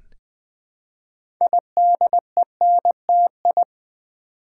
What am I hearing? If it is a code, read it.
Morse code, heard as a keyed tone, about 20 wpm, 700 Hz. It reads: IDENTI
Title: Build A Word - 2 Letters At A Time - 8-Letter Words - 20wpm